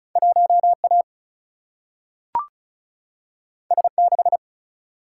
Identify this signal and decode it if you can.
Morse code, heard as a keyed tone, about 36 wpm, 700 Hz. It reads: 1A S6
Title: Single-Letter Single-Number 35wpm